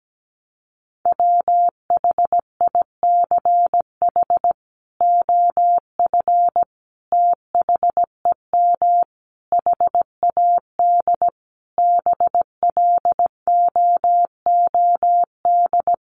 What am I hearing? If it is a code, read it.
Morse code, heard as a keyed tone, about 17 wpm, 700 Hz. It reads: WHICH OF THEM HAD BLOOD
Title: Sentences from Top 1100 Words 17wpm